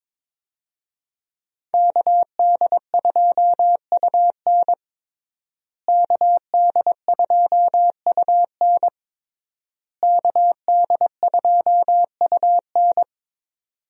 Morse code, heard as a keyed tone, about 22 wpm, 700 Hz. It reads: KD2UN KD2UN KD2UN
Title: USA Call Signs - 5 Characters - Repeated 3x - 22wpm